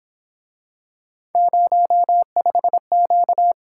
Morse code, heard as a keyed tone, about 27 wpm, 700 Hz. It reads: 05Q
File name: Speed-Racing - Three Letter-Number Combinations - 22wpm